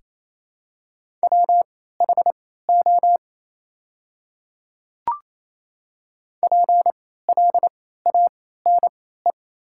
Morse code, heard as a keyed tone, about 28 wpm, spaced as for 16 wpm, 700 Hz. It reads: WHO PLANE
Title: 3x Character Spacing - Top 500 Words - 28wpm